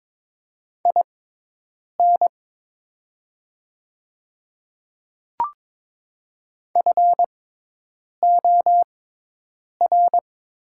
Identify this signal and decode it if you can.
Morse code, heard as a keyed tone, about 22 wpm, spaced as for 8 wpm, 700 Hz. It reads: IN FOR
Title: 6x Character Spacing - Top 200 Words - 22wpm